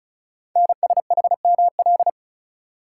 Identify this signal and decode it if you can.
Morse code, heard as a keyed tone, about 36 wpm, 690 Hz. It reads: NSHML
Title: Code Groups - 5 Letters - 35wpm